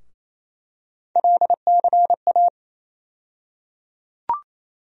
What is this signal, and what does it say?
Morse code, heard as a keyed tone, about 28 wpm, 700 Hz. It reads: LCA
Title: USA Call Sign Postfixes - 3 Letters 28wpm